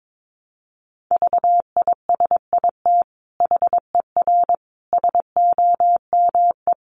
Morse code, heard as a keyed tone, about 22 wpm, 700 Hz. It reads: VISIT HER SOME
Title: Sentences from Top 900 Words 22wpm